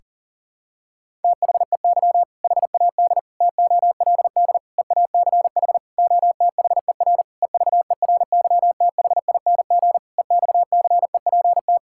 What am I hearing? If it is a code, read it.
Morse code, heard as a keyed tone, about 40 wpm, 700 Hz. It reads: THEY HAD TOLD EACH OTHER EVERYTHING EXCEPT
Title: Sentences from Top 1600 Words 40wpm